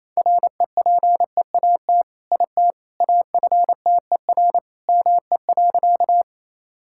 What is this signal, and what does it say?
Morse code, heard as a keyed tone, about 28 wpm, 700 Hz. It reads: REPEAT IT AFTER ME.
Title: Sentences from Top 1600 Words 28wpm